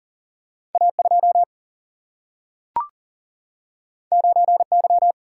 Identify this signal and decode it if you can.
Morse code, heard as a keyed tone, about 39 wpm, 700 Hz. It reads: A2 9Y
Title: Single-Letter Single-Number 40wpm